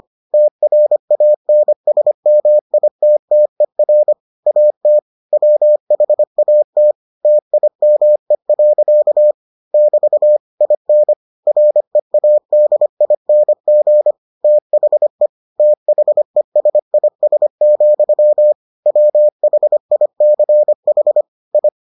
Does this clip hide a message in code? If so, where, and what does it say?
Yes: Morse code from 0.3 to 21.7 s, heard as a keyed tone, about 25 wpm, 600 Hz. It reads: TRANSMITTER AT WHAT TIME. <BT> IN READING THE THESIS, WHICH I